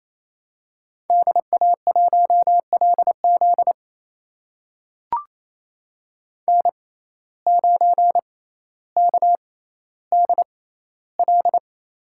Morse code, heard as a keyed tone, about 28 wpm, spaced as for 14 wpm, 700 Hz. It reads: DA1LZ N9KDL
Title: USA Call Signs - 5 Characters - 6x Character Spacing - 28wpm